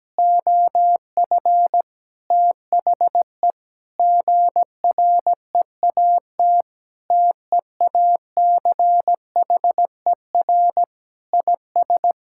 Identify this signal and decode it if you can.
Morse code, heard as a keyed tone, about 17 wpm, 700 Hz. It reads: OF THE GREAT TEACHER IS
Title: Sentences from Top 1600 Words 17wpm